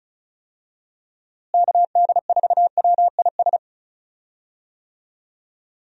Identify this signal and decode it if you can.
Morse code, heard as a keyed tone, about 34 wpm, 700 Hz. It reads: KD4WIS
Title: USA Call Signs - 6 Characters - Repeated 2x - 35wpm